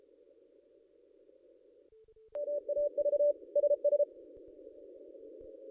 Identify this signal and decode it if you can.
Morse code, heard as a keyed tone, about 33 wpm, 580 Hz. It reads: MAV SS
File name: contests / ss-cw-k5tr-op-n5rz -1400